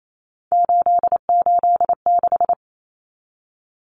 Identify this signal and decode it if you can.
Morse code, heard as a keyed tone, about 28 wpm, 700 Hz. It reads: 886
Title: Numbers - 3-Digit 28wpm